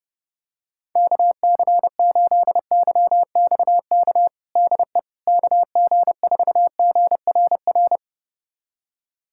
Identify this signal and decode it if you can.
Morse code, heard as a keyed tone, about 30 wpm, 700 Hz. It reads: KC8YXK DE KG4GRR
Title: USA Callsign DE Callsign - 6 Characters - 30wpm